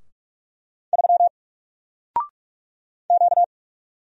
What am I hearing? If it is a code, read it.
Morse code, heard as a keyed tone, about 44 wpm, 700 Hz. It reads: 3 Q
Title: Single Letter-Number Alt 45wpm